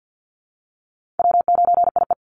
Morse code, heard as a keyed tone, about 50 wpm, 700 Hz. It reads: W9IE
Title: Call Signs Repeated 2x - 50wpm